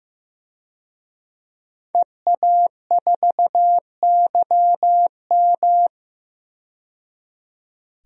Morse code, heard as a keyed tone, about 15 wpm, 700 Hz. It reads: EA4YM
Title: DXCC-100 Call Signs - 5 Characters - 15wpm